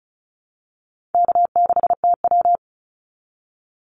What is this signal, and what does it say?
Morse code, heard as a keyed tone, about 34 wpm, 700 Hz. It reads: K6TW
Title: NA SOTA Activator Callsigns 35wpm